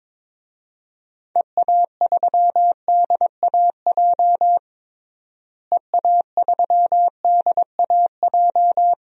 Morse code, heard as a keyed tone, about 22 wpm, 700 Hz. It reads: EA3DAJ EA3DAJ
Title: DXCC-100 Call Signs - 6 Characters - Repeated 3x - 22wpm